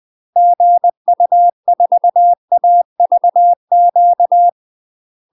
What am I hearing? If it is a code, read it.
Morse code, heard as a keyed tone, about 20 wpm, 700 Hz. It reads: GU4AVQ
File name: UK MASTER QSOs TRAINING / 20WPM / UK MASTER QSOs 0028